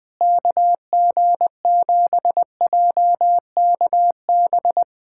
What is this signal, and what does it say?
Morse code, heard as a keyed tone, about 20 wpm, 700 Hz. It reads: KG7JKB
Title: USA Callsign DE Callsign 20wpm